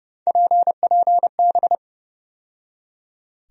Morse code, heard as a keyed tone, about 30 wpm, 700 Hz. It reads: PPB
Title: USA Call Sign Postfixes - 3 Letters 30wpm